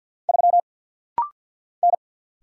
Morse code, heard as a keyed tone, about 50 wpm, 690 Hz. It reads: 3 N
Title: Single Letter-Number Alt - Rapid-Fire 50wpm